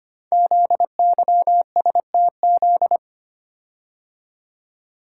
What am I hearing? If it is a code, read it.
Morse code, heard as a keyed tone, about 25 wpm, 700 Hz. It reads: ZYSTZ
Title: Code Groups - 5 Letters - 25wpm